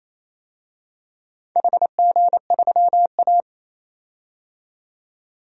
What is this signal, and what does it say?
Morse code, heard as a keyed tone, about 28 wpm, 700 Hz. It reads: HG3A
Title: DXCC-100 Call Signs - 4 Characters - 28wpm